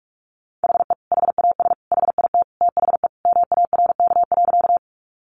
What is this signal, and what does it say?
Morse code, heard as a keyed tone, about 44 wpm, 700 Hz. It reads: HE HAS HIT THE MARK.
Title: Sentences from Top 1200 Words 45wpm